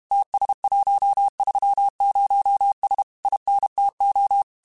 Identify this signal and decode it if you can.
Morse code, heard as a keyed tone, about 32 wpm, 800 Hz. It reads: TS130S INTO
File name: QSOs / QSOs-mp3s / WD0BGP 1@30